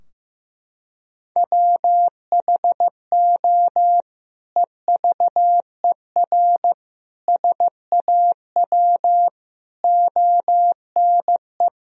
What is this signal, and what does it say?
Morse code, heard as a keyed tone, about 15 wpm, 700 Hz. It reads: WHO EVER SAW ONE
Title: Sentences from Top 1000 Words - Review Part 2 - 15wpm